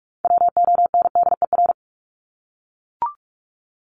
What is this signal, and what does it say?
Morse code, heard as a keyed tone, about 44 wpm, 700 Hz. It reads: WONDER
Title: Top 500 Words Repeated 45wpm